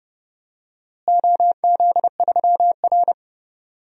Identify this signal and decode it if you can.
Morse code, heard as a keyed tone, about 30 wpm, 700 Hz. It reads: OZ3R
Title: DXCC-100 Call Signs - 4 Characters - Repeated 2x - 30wpm